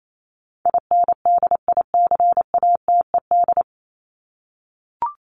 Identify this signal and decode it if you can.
Morse code, heard as a keyed tone, about 28 wpm, 700 Hz. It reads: INDICATED
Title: ED Suffix - Top 100 Words with Base Words 28wpm